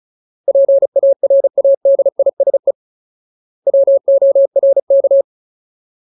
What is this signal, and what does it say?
Morse code, heard as a keyed tone, about 34 wpm, 550 Hz. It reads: PARADISE WORK
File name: Ultimate Word List / 35WPM Ultimate